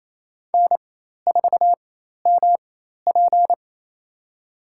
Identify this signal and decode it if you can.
Morse code, heard as a keyed tone, about 28 wpm, spaced as for 13 wpm, 700 Hz. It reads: N4MP
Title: USA Call Signs - 4x Character Spacing - 28wpm